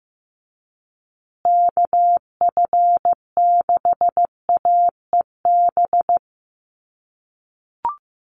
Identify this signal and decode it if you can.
Morse code, heard as a keyed tone, about 15 wpm, 700 Hz. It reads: KF6AEB
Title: USA Call Signs Repeated 3x 15wpm